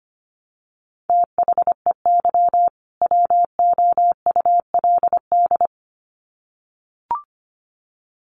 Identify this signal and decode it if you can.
Morse code, heard as a keyed tone, about 25 wpm, 700 Hz. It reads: THEY WOULD
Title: Sets of 2 Words 25wpm